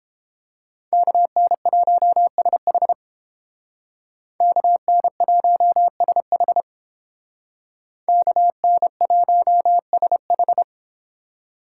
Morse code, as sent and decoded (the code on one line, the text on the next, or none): -.- -. .---- ... .... / -.- -. .---- ... .... / -.- -. .---- ... ....
KN1SH KN1SH KN1SH